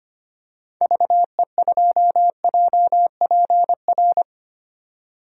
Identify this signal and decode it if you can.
Morse code, heard as a keyed tone, about 25 wpm, 700 Hz. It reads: VE2JPR
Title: DXCC-100 Call Signs - 6 Characters - 25wpm